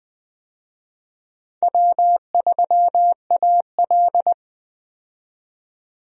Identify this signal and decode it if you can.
Morse code, heard as a keyed tone, about 20 wpm, 700 Hz. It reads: W3AL
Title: USA Call Signs - 4 Characters - 20wpm